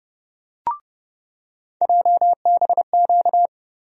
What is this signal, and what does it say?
Morse code, heard as a keyed tone, about 30 wpm, 700 Hz. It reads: JBQ
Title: USA Call Sign Postfixes - 3 Letters 30wpm